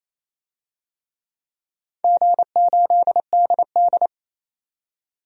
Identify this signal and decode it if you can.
Morse code, heard as a keyed tone, about 27 wpm, 690 Hz. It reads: G8DD
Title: DXCC-100 Call Signs - 4 Characters - 28wpm